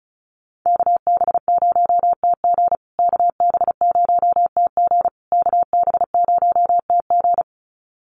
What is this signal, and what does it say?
Morse code, heard as a keyed tone, about 34 wpm, 700 Hz. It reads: KB0TG KB0TG KB0TG
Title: USA Call Signs Repeated 3x 35wpm